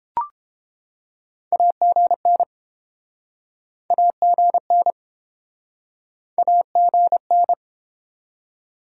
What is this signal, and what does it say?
Morse code, heard as a keyed tone, about 30 wpm, 700 Hz. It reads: AGN AGN AGN